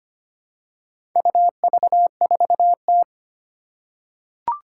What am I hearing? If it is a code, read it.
Morse code, heard as a keyed tone, about 25 wpm, 700 Hz. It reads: UV4T
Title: Four Challenging Letter-Number Combinations 25wpm